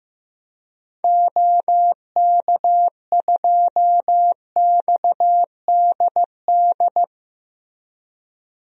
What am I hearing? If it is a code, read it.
Morse code, heard as a keyed tone, about 15 wpm, 700 Hz. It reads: OK2XDD
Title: DXCC-100 Call Signs - 6 Characters - Repeated 2x - 15wpm